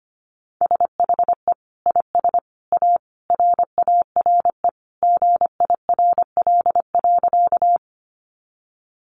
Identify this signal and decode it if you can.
Morse code, heard as a keyed tone, about 25 wpm, 700 Hz. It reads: SHE IS A RARE GIRL.
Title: Sentences from Top 1100 Words 25wpm